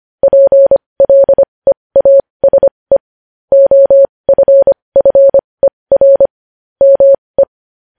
Morse code, heard as a keyed tone, about 25 wpm, spaced as for 20 wpm, 550 Hz. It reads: PLEASE OFFER ME
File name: podcast / News headlines-1x-25x20